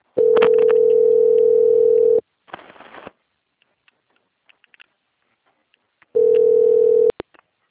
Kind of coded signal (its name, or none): telephone call progress tone